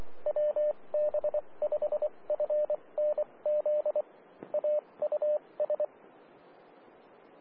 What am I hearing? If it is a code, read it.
Morse code, heard as a keyed tone, about 24 wpm, spaced as for 20 wpm, 600 Hz. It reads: WB5FNZ AUS
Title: Austin, TX Amateur Repeaters